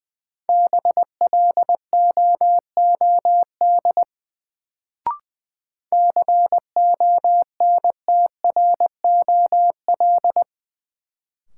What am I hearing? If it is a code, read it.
Morse code, heard as a keyed tone, about 20 wpm, 700 Hz. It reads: BLOOD CONTROL
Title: Sentences from Top 1000 Words - Review Part 2 - New Word Drill - 20wpm